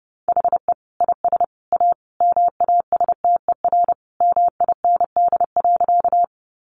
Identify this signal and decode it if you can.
Morse code, heard as a keyed tone, about 30 wpm, 700 Hz. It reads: HE IS A MASTER MIND.
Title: Sentences from Top 1300 Words 30wpm